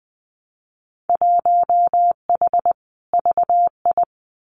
Morse code, heard as a keyed tone, about 19 wpm, 700 Hz. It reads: 1H VI
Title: Winter Field Day Exchanges Encore 20wpm